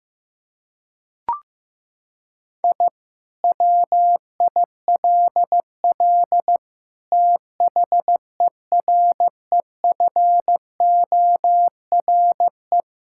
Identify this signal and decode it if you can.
Morse code, heard as a keyed tone, about 15 wpm, 700 Hz. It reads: I WILL THEREFORE
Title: Sentences from Top 1000 Words - Review Part 5 - 15wpm